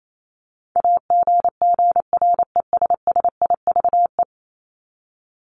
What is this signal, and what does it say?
Morse code, heard as a keyed tone, about 28 wpm, 700 Hz. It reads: AGGRESSIVE
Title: Top 1501-1750 Words 28wpm